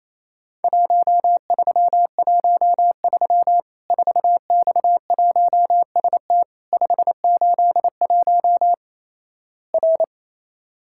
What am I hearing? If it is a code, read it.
Morse code, heard as a keyed tone, about 28 wpm, 700 Hz. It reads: 1313 4X1ST 581 R
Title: Worked All Europe DX Contest - QTC Exchange - 28wpm